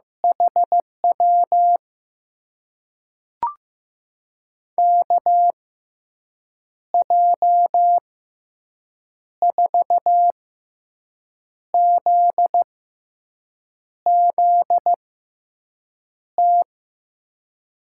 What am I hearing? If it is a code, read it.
Morse code, heard as a keyed tone, about 15 wpm, spaced as for 6 wpm, 700 Hz. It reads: HW KJ4ZZT
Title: USA Call Signs - 6 Characters - 6x Character Spacing - 15wpm